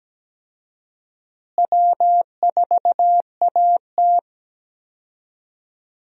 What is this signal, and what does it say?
Morse code, heard as a keyed tone, about 17 wpm, 700 Hz. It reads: W4AT
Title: USA Call Signs - 4 Characters - Repeated 2x - 17wpm